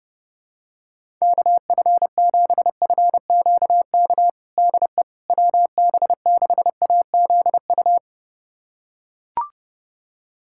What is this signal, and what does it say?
Morse code, heard as a keyed tone, about 32 wpm, 700 Hz. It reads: KF7FQK DE WB6AZU
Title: USA Callsign DE Callsign - 6 Characters - 30wpm